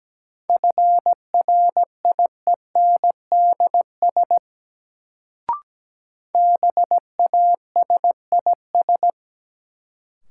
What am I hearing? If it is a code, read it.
Morse code, heard as a keyed tone, about 17 wpm, 700 Hz. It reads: FRIENDS BASIS
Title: Sentences from Top 1000 Words - Review Part 5 - New Word Drill - 17wpm